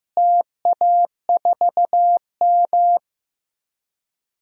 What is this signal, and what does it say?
Morse code, heard as a keyed tone, about 15 wpm, 700 Hz. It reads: TA4M
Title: DXCC-100 Call Signs - 4 Characters - Repeated 3x - 15wpm